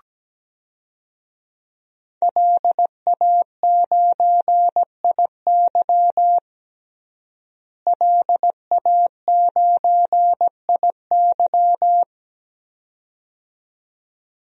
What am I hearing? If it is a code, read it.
Morse code, heard as a keyed tone, about 17 wpm, 700 Hz. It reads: LA9IY LA9IY
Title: DXCC-100 Call Signs - 5 Characters - Repeated 2x - 17wpm